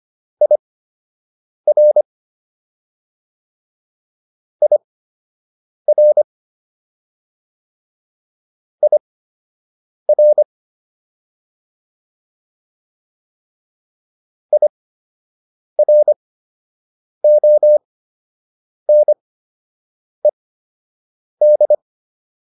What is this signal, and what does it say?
Morse code, heard as a keyed tone, about 25 wpm, spaced as for 7 wpm, 600 Hz. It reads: IR IR IR IRONED
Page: fundamental-practice-files